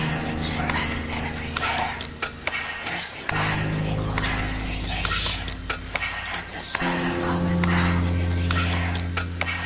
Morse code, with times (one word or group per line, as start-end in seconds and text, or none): none